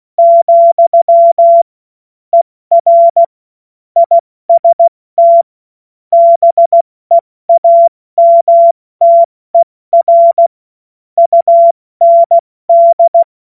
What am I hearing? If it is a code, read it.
Morse code, heard as a keyed tone, about 16 wpm, 680 Hz. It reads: , ER IST BEAMTER UND
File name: Witz-14-Oktober-2025-70-F80